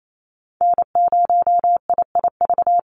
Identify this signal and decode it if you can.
Morse code, heard as a keyed tone, about 28 wpm, 700 Hz. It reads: N0IIV
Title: USA Call Signs 28wpm